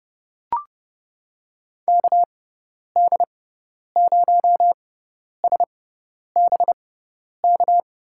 Morse code, heard as a keyed tone, about 30 wpm, spaced as for 10 wpm, 700 Hz. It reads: KD0SBK
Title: USA Call Signs - 6 Characters - 6x Character Spacing - 30wpm